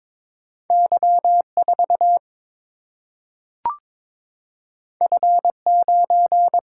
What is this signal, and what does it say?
Morse code, heard as a keyed tone, about 22 wpm, 700 Hz. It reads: Y 4 F 9
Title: Single-Letter Single-Number 22wpm